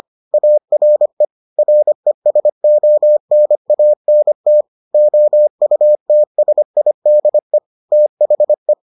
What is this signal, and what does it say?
Morse code, heard as a keyed tone, about 25 wpm, 600 Hz. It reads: ARE RESONANT OUTSIDE THE